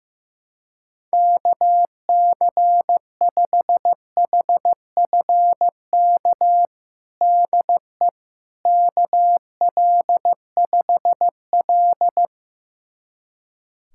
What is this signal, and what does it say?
Morse code, heard as a keyed tone, about 15 wpm, 700 Hz. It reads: KC5HFK DE KL5L
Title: USA Callsign DE Callsign 15wpm